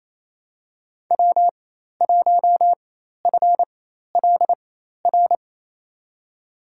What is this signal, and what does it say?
Morse code, heard as a keyed tone, about 28 wpm, spaced as for 13 wpm, 700 Hz. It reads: W1FLR
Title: USA Call Signs - 5 Characters - 4x Character Spacing - 28wpm